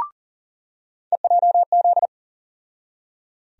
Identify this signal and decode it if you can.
Morse code, heard as a keyed tone, about 37 wpm, 700 Hz. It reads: EJZ
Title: USA Call Sign Postfixes - 3 Letters 40wpm